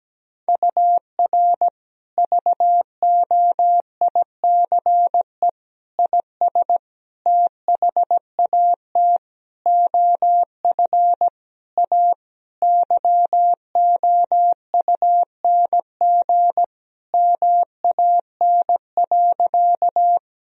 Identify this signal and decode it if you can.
Morse code, heard as a keyed tone, about 17 wpm, 700 Hz. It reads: UR VOICE IS THAT OF A YOUNG MAN.
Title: Sentences from Top 500 Words and Abbreviations 17wpm